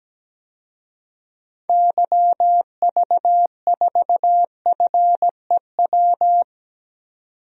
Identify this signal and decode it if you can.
Morse code, heard as a keyed tone, about 17 wpm, 700 Hz. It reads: YV4FEW
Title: DXCC-100 Call Signs - 6 Characters - 17wpm